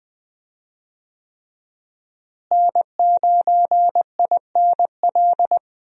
Morse code, heard as a keyed tone, about 20 wpm, 700 Hz. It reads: N9INL
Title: USA Call Signs - 5 Characters - Repeated 3x - 20wpm